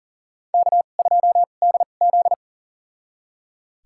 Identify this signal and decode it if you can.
Morse code, heard as a keyed tone, about 37 wpm, spaced as for 28 wpm, 700 Hz. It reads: K2DZ
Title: USA Call Signs - 2x Character Spacing - 40wpm